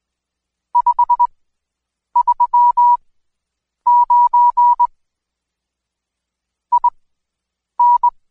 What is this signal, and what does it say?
Morse code, heard as a keyed tone, about 21 wpm, spaced as for 8 wpm, 970 Hz. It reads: 539 IN
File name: Code Quick / AudioDownld2 / Seg114-Day19-DayDrill 19